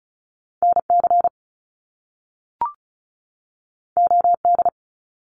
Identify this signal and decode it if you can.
Morse code, heard as a keyed tone, about 36 wpm, 700 Hz. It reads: NC OD
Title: Top Two Letter Combinations in Words 35wpm